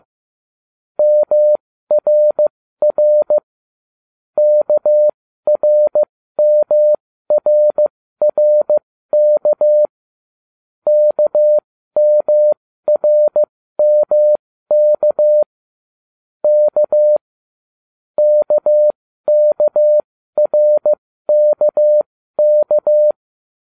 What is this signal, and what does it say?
Morse code, heard as a keyed tone, about 15 wpm, spaced as for 13 wpm, 600 Hz. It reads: MRR KRMRRK KMRMK K KKRKK